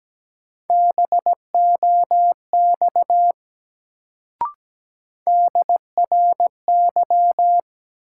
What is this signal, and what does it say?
Morse code, heard as a keyed tone, about 17 wpm, 700 Hz. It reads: BOX DRY
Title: Sentences from Top 500 Words - New Word Drill - 17wpm